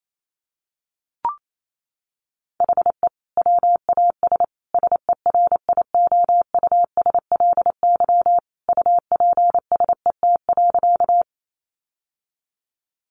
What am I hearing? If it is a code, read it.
Morse code, heard as a keyed tone, about 28 wpm, 700 Hz. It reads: HE WAS SERIOUSLY UPSET.
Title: Sentences from Top 1200 Words 28wpm